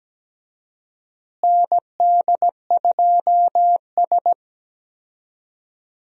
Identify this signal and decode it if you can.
Morse code, heard as a keyed tone, about 17 wpm, 700 Hz. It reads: ND2S
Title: USA Call Signs - 4 Characters - Repeated 3x - 17wpm